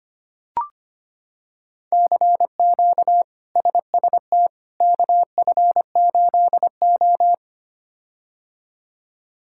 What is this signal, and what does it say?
Morse code, heard as a keyed tone, about 25 wpm, 700 Hz. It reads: CQ SST KF8O
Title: K1USN SST QSO Practice - 25wpm